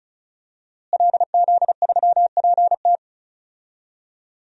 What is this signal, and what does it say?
Morse code, heard as a keyed tone, about 36 wpm, 700 Hz. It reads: LZ3PT
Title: DXCC-100 Call Signs - 5 Characters - Repeated 3x - 35wpm